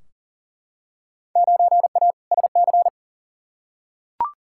Morse code, heard as a keyed tone, about 39 wpm, 700 Hz. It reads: 9A SC
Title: Field Day Exchanges Encore - 40wpm